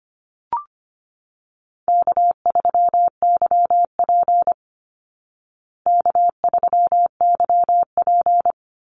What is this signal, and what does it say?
Morse code, heard as a keyed tone, about 25 wpm, 700 Hz. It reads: K3YP K3YP
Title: Call Signs Repeated 2x - 25wpm